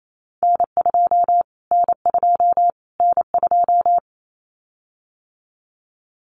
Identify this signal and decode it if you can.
Morse code, heard as a keyed tone, about 28 wpm, 700 Hz. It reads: N2 N2 N2
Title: Callsign Prefixes for Top 25 DXCC - Repeated 3x - 28wpm